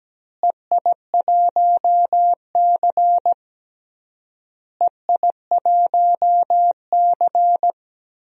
Morse code, heard as a keyed tone, about 17 wpm, 700 Hz. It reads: EI1C EI1C
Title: DXCC-100 Call Signs - 4 Characters - Repeated 3x - 17wpm